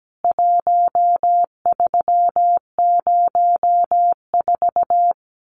Morse code, heard as a keyed tone, about 17 wpm, 700 Hz. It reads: 1304
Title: CWT Exchange 17wpm